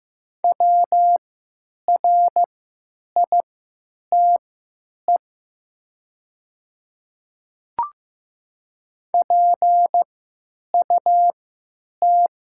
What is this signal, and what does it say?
Morse code, heard as a keyed tone, about 15 wpm, spaced as for 9 wpm, 700 Hz. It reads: WRITE PUT
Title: 3x Character Spacing - Top 200 Words - 15wpm